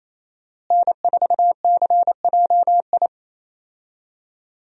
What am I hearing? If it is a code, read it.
Morse code, heard as a keyed tone, about 28 wpm, 700 Hz. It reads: N4CJI
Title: USA Call Signs - 5 Characters - Repeated 2x - 28wpm